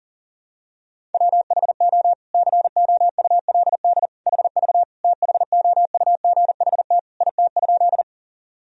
Morse code, heard as a keyed tone, about 40 wpm, 700 Hz. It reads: WHO COULD HV THOUGHT IT?
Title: Sentences from Top 300 Words and Abbreviations 40wpm